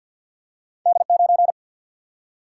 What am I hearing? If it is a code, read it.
Morse code, heard as a keyed tone, about 52 wpm, 700 Hz. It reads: D 9
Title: Single-Letter Single-Number 50wpm